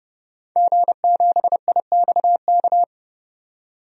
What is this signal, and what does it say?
Morse code, heard as a keyed tone, about 30 wpm, 700 Hz. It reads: G7IXK
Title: DXCC-100 Call Signs - 5 Characters - 30wpm